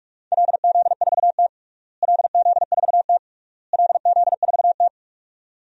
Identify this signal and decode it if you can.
Morse code, heard as a keyed tone, about 46 wpm, 700 Hz. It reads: LZ4T LZ4T LZ4T
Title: DXCC-100 Call Signs - 4 Characters - Repeated 3x - 45wpm